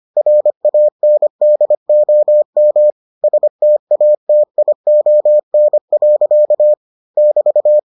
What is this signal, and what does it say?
Morse code, heard as a keyed tone, about 25 wpm, 600 Hz. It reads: RANDOM STATION. <BT>